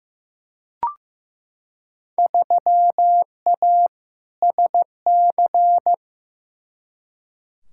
Morse code, heard as a keyed tone, about 16 wpm, 700 Hz. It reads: 3A SC
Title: Field Day Exchanges - 15wpm